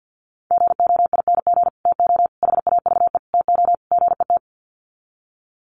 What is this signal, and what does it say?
Morse code, heard as a keyed tone, about 50 wpm, 700 Hz. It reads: GOING TO HAVE TO GET
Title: Sets of 5 Words Encore2 50wpm